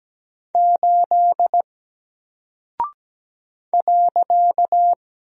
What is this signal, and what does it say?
Morse code, heard as a keyed tone, about 17 wpm, 700 Hz. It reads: 8 .
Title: Single Character (Letter:Number:Punctuation:Prosign) 17wpm